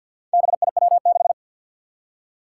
Morse code, heard as a keyed tone, about 51 wpm, 700 Hz. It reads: BIW6
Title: Four Challenging Letter-Number Combinations 50wpm